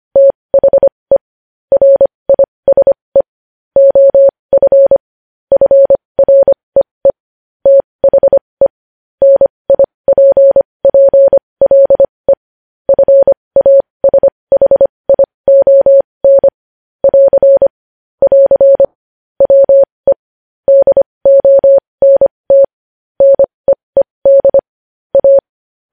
Morse code, heard as a keyed tone, about 25 wpm, spaced as for 20 wpm, 550 Hz. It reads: THE RISE OF FREE THE NIPPLE FASHION <AR> <AR> WE DONT NEED A
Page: news-headlines-in-morse-code-at-20-wpm